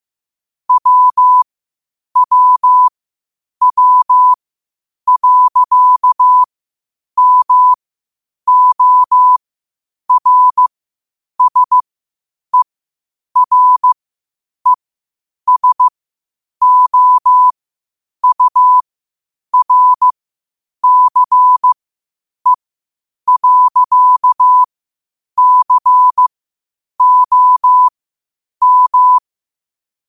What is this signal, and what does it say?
Morse code, heard as a keyed tone, about 15 wpm, spaced as for 8 wpm, 1000 Hz. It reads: WWW.MORSERESOURCE.COM
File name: qotdfiles / 8wpm